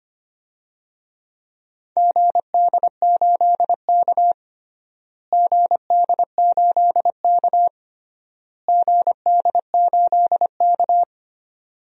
Morse code, heard as a keyed tone, about 25 wpm, 700 Hz. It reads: GD8K GD8K GD8K
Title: DXCC-100 Call Signs - 4 Characters - Repeated 3x - 25wpm